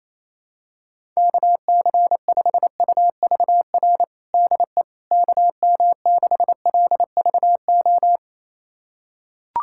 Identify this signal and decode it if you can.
Morse code, heard as a keyed tone, about 28 wpm, 700 Hz. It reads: KC5UVR DE KM6LVO
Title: USA Callsign DE Callsign 28wpm